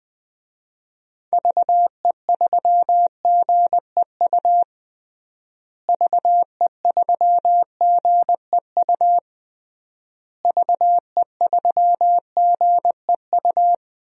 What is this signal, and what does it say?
Morse code, heard as a keyed tone, about 20 wpm, 700 Hz. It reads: VE3GEU VE3GEU VE3GEU
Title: DXCC-100 Call Signs - 6 Characters - Repeated 3x - 20wpm